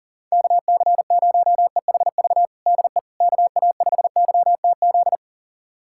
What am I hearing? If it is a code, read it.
Morse code, heard as a keyed tone, about 40 wpm, 700 Hz. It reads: KC0EHV DE KA5YTZ
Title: USA Callsign DE Callsign - 6 Characters - 40wpm